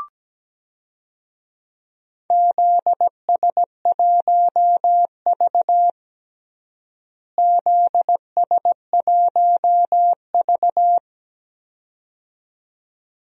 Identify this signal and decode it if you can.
Morse code, heard as a keyed tone, about 17 wpm, 700 Hz. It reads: ZS1V ZS1V
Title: DXCC-100 Call Signs - 4 Characters - Repeated 2x - 17wpm